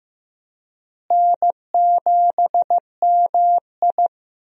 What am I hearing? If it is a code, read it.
Morse code, heard as a keyed tone, about 15 wpm, 700 Hz. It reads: N7MI
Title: USA Call Signs - 4 Characters - Repeated 2x - 15wpm